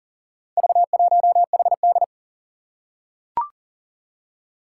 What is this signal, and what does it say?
Morse code, heard as a keyed tone, about 40 wpm, 700 Hz. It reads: V1HD
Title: Four Challenging Letter-Number Combinations 40wpm